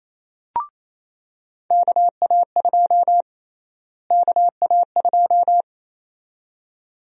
Morse code, heard as a keyed tone, about 28 wpm, 700 Hz. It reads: KA2 KA2
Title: USA Call Sign Prefixes - Repeated 2x - 28wpm